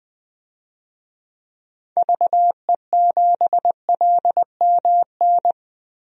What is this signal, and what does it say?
Morse code, heard as a keyed tone, about 20 wpm, 700 Hz. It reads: VE7LMN
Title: DXCC-100 Call Signs - 6 Characters - 20wpm